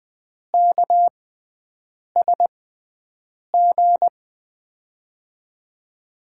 Morse code, heard as a keyed tone, about 20 wpm, spaced as for 7 wpm, 700 Hz. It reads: KSG
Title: USA Call Signs - 6 Characters - 6x Character Spacing - 20wpm